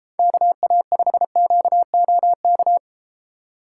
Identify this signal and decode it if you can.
Morse code, heard as a keyed tone, about 32 wpm, 700 Hz. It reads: KA5QOK